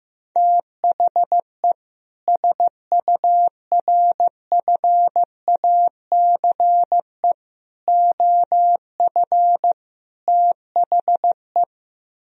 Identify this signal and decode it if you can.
Morse code, heard as a keyed tone, about 15 wpm, 700 Hz. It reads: THE SURFACE OF THE
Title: Speed-Racing - Sets of 4 Words 15wpm